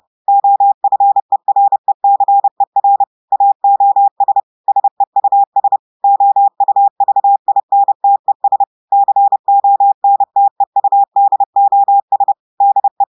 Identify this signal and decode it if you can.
Morse code, heard as a keyed tone, about 30 wpm, 800 Hz. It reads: OFERECER AOS SEUS OUVINTES CONTEUDOS DE
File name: portugais texte en clair 2 vitesse 30 mots minutes